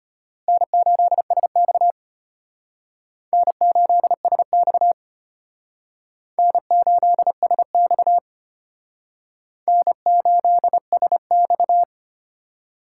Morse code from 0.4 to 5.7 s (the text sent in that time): N8SX N8SX